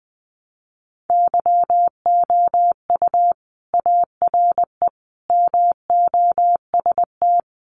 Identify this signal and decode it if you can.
Morse code, heard as a keyed tone, about 20 wpm, 700 Hz. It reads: YOU ARE MOST
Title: Sentences from Top 200 Words 20wpm